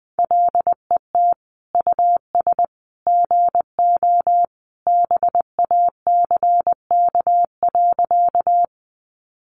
Morse code, heard as a keyed tone, about 20 wpm, 700 Hz. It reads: LET US GO BACK.
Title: Sentences from Top 300 Words 20wpm